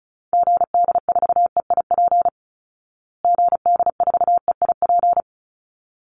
Morse code, heard as a keyed tone, about 34 wpm, 700 Hz. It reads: GD4EIP GD4EIP
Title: Call Signs Repeated 3x - 35wpm